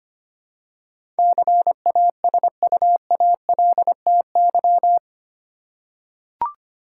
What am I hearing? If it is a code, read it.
Morse code, heard as a keyed tone, about 25 wpm, 700 Hz. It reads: CASUALTY
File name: 8 Letter Words - First 3 Letters Given - 25wpm